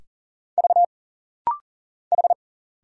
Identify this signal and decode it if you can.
Morse code, heard as a keyed tone, about 40 wpm, 700 Hz. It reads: V H
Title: Single Letters - Rapid-Fire 40wpm